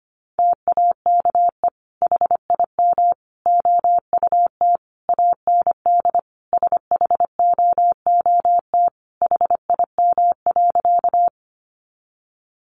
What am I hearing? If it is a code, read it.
Morse code, heard as a keyed tone, about 25 wpm, 700 Hz. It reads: TAKE HIM OUT AND SHOOT HIM.
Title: Sentences from Top 1200 Words 25wpm